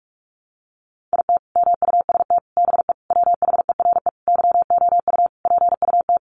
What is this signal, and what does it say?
Morse code, heard as a keyed tone, about 44 wpm, 700 Hz. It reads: IT MUST BE WHERE YOU PUT
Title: Sentences from Top 200 Words 45wpm